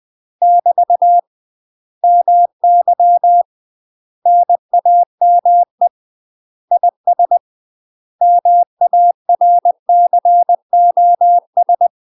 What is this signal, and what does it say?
Morse code, heard as a keyed tone, about 20 wpm, 700 Hz. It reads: <BT> MY NAME IS MARCOS
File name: UK MASTER QSOs TRAINING / 20WPM / UK MASTER QSOs 0066